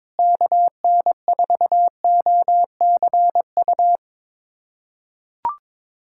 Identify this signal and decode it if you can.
Morse code, heard as a keyed tone, about 22 wpm, 700 Hz. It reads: KN4OCU
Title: USA Callsign DE Callsign - 6 Characters - 22wpm